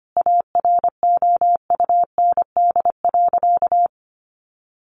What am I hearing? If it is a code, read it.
Morse code, heard as a keyed tone, about 25 wpm, 700 Hz. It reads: AROUND.
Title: Sentences from Top 600 Words 25wpm